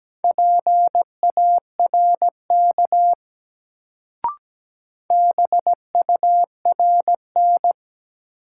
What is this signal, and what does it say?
Morse code, heard as a keyed tone, about 17 wpm, 700 Hz. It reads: PARK BURN
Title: Sentences from Top 1300 Words - New Word Drill Alt - 17wpm